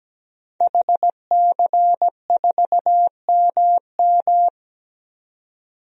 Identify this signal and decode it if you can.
Morse code, heard as a keyed tone, about 17 wpm, 700 Hz. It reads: HC4MM
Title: DXCC-100 Call Signs - 5 Characters - Repeated 2x - 17wpm